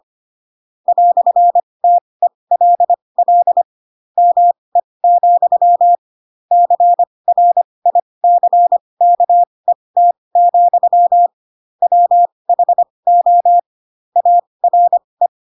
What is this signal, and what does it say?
Morse code, heard as a keyed tone, about 25 wpm, spaced as for 20 wpm, 700 Hz. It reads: "TELL ME, CRICKET, WHO ARE
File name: Pinocchio / Pinocchio ch4 15 25 ik1hge